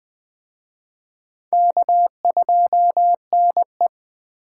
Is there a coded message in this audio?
Yes: Morse code from 1.5 to 3.9 s, heard as a keyed tone, about 21 wpm, 700 Hz. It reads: K2NE